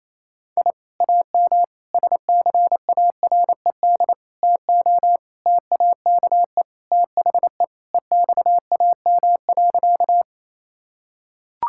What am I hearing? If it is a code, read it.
Morse code, heard as a keyed tone, about 27 wpm, 700 Hz. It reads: I AM SCARED TO TAKE THE EXAM.
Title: Sentences from Top 1000 Words - Review Part 5 - 28wpm